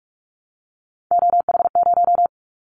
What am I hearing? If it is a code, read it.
Morse code, heard as a keyed tone, about 46 wpm, 700 Hz. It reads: OH0
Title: Callsign Prefixes for Top 100 DXCC - 45wpm